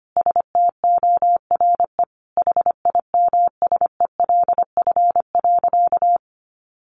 Morse code, heard as a keyed tone, about 25 wpm, 700 Hz. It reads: STORE HIMSELF.
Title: Easy Phrases 25wpm